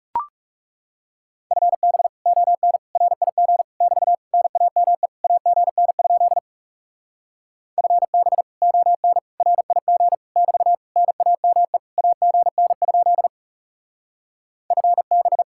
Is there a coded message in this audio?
Yes: Morse code from 1.5 to 15.4 s, heard as a keyed tone, about 39 wpm, 700 Hz. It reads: FB ON RIG <BT> NAME AGN? FB ON RIG <BT> NAME AGN? FB